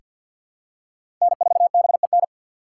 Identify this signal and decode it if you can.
Morse code, heard as a keyed tone, about 48 wpm, 700 Hz. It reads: N4BEN
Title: USA Call Signs - 6x Character Spacing - 50wpm